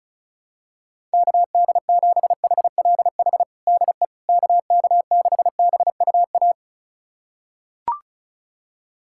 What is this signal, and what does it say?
Morse code, heard as a keyed tone, about 34 wpm, 700 Hz. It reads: KD7HLH DE KK6BUA
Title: USA Callsign DE Callsign - 6 Characters - 35wpm